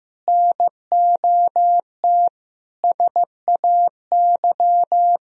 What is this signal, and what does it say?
Morse code, heard as a keyed tone, about 16 wpm, 700 Hz. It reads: NOT SAY
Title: Sentences from Top 1000 Words - Review Part 4 - 15wpm